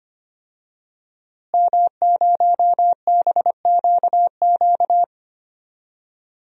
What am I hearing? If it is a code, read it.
Morse code, heard as a keyed tone, about 25 wpm, 700 Hz. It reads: M0BQQ
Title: DXCC-100 Call Signs - 5 Characters - Repeated 2x - 25wpm